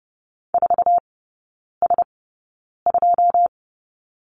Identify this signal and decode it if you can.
Morse code, heard as a keyed tone, about 30 wpm, spaced as for 9 wpm, 700 Hz. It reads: 4S2
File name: 081 Lesson 14e - Number 2 - Three Characters 30wpm